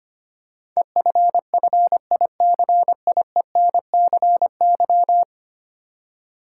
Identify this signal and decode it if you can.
Morse code, heard as a keyed tone, about 25 wpm, 700 Hz. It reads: EFFICIENCY
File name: 10 Letter Words - First 3 Letters Given - 25wpm